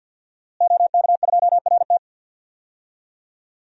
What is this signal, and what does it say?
Morse code, heard as a keyed tone, about 46 wpm, 700 Hz. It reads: OK2RT
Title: DXCC-100 Call Signs - 5 Characters - Repeated 3x - 50wpm